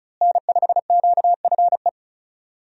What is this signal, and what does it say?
Morse code, heard as a keyed tone, about 34 wpm, 700 Hz. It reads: N5QFE
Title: USA Call Signs - 5 Characters - Repeated 3x - 35wpm